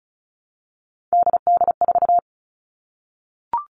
Morse code, heard as a keyed tone, about 36 wpm, 700 Hz. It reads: DD4
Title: Three Challenging Letter-Number Combinations 35wpm